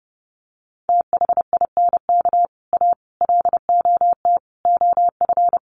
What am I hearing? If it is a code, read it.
Morse code, heard as a keyed tone, about 30 wpm, 700 Hz. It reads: THINK A LOT OF
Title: Sets of 5 Words 30wpm